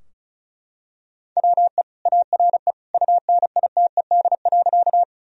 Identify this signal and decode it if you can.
Morse code, heard as a keyed tone, about 36 wpm, 700 Hz. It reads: WE ARE UNITED.
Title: Sentences from Top 1000 Words - Review Part 4 - 35wpm